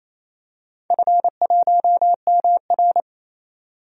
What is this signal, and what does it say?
Morse code, heard as a keyed tone, about 28 wpm, 700 Hz. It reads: F1MR
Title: DXCC-100 Call Signs - 4 Characters - 28wpm